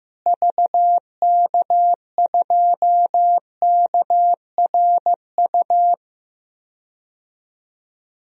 Morse code, heard as a keyed tone, about 15 wpm, 700 Hz. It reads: VK2KRU
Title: DXCC-100 Call Signs - 6 Characters - Repeated 2x - 15wpm